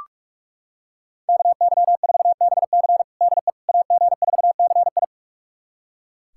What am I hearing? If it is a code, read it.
Morse code, heard as a keyed tone, about 43 wpm, 700 Hz. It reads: KY4BC DE AG4CI
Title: USA Callsign DE Callsign - 5 Characters - 45wpm